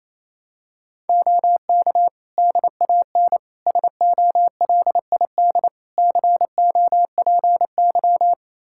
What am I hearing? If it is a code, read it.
Morse code, heard as a keyed tone, about 27 wpm, 700 Hz. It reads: OK DAN SOLID COPY
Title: Practice QSO - Exchange 3 - More Info - 28wpm